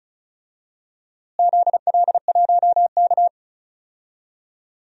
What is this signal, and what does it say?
Morse code, heard as a keyed tone, about 36 wpm, 700 Hz. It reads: ZL1K
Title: DXCC-100 Call Signs - 4 Characters - Repeated 2x - 35wpm